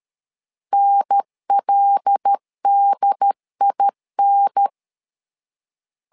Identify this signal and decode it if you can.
Morse code, heard as a keyed tone, about 13 wpm, 800 Hz. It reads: NLDIN
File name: LEZIONI / 5 LBVC / 5 LBV 12wpm 1855